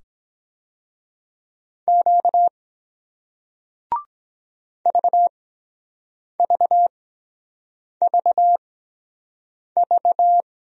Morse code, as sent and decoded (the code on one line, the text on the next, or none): --.- / ...- ...- ...- ...-
Q VVVV